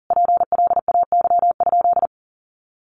Morse code, heard as a keyed tone, about 40 wpm, 700 Hz. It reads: PLAY?
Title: Sentences from Top 1400 Words 40wpm